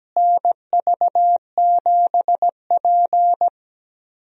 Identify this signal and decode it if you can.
Morse code, heard as a keyed tone, about 17 wpm, 700 Hz. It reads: NV7P
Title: USA Call Signs - 4 Characters - Repeated 3x - 17wpm